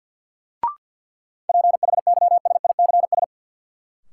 Morse code, heard as a keyed tone, about 46 wpm, 700 Hz. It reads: PHYSICS
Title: Sentences from Top 1000 Words - Review Part 5 - New Word Drill - 50wpm